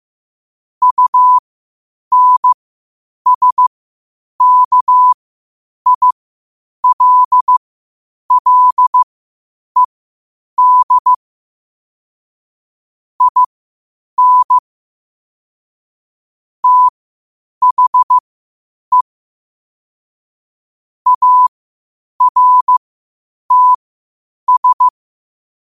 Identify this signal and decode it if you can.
Morse code, heard as a keyed tone, about 15 wpm, spaced as for 8 wpm, 1000 Hz. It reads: UNSKILLED IN THE ARTS
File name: qotdfiles / 8wpm